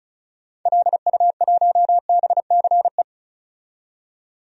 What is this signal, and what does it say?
Morse code, heard as a keyed tone, about 34 wpm, 700 Hz. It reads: LU1BCE
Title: DXCC-100 Call Signs - 6 Characters - 35wpm